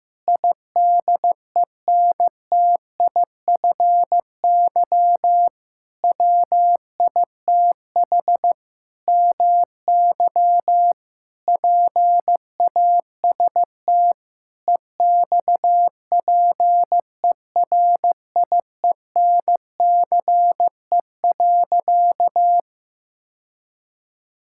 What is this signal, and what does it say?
Morse code, heard as a keyed tone, about 15 wpm, 700 Hz. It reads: IDENTIFY WITH MY PAST EXPERIENCE.
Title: Sentences from Top 1000 Words - Review Part 6 - 15wpm